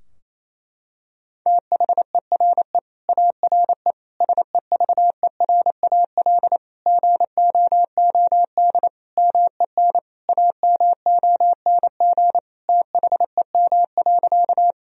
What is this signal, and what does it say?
Morse code, heard as a keyed tone, about 27 wpm, 700 Hz. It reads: THERE ARE SEVERAL GOOD MEN AMONG THEM.
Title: Sentences from Top 1000 Words - Review Part 2 - 28wpm